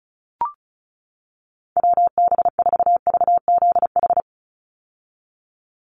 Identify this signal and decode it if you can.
Morse code, heard as a keyed tone, about 34 wpm, 700 Hz. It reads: WB4VZH
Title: USA Call Signs 35wpm